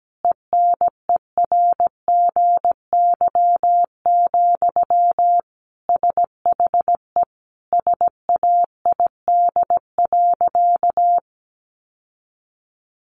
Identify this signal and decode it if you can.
Morse code, heard as a keyed tone, about 17 wpm, 700 Hz. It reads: ENERGY, SHE SAID.
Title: Sentences from Top 900 Words 17wpm